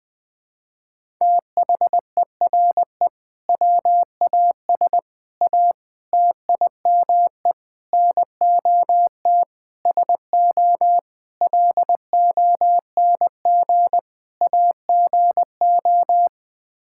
Morse code, heard as a keyed tone, about 20 wpm, 700 Hz. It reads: THERE WAS A TIME NOT SO LONG AGO
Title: Sets of 8 Words - 20wpm